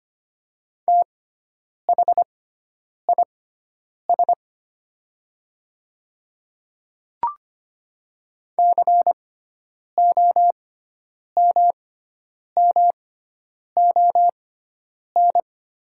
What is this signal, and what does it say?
Morse code, heard as a keyed tone, about 25 wpm, spaced as for 9 wpm, 700 Hz. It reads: THIS COMMON
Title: 6x Character Spacing - Top 500 Words - 25wpm